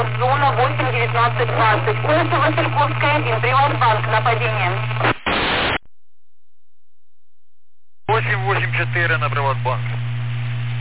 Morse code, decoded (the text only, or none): none